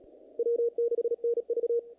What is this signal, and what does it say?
Morse code, heard as a keyed tone, about 34 wpm, 470 Hz. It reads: W6NV
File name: contests / naqp-cw-k5tr-op-k5pi- -2100